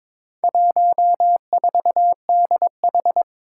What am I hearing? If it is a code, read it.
Morse code, heard as a keyed tone, about 22 wpm, 700 Hz. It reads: 14DH
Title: Four Challenging Letter-Number Combinations 22wpm